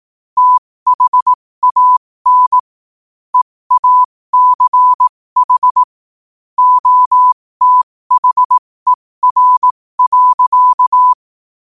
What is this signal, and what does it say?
Morse code, heard as a keyed tone, about 18 wpm, spaced as for 15 wpm, 1000 Hz. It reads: THAN EACH OTHER.
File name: qotdfiles / 15wpm